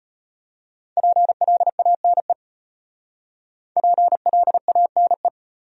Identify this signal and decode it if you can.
Morse code, heard as a keyed tone, about 34 wpm, 700 Hz. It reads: PLANE PLANE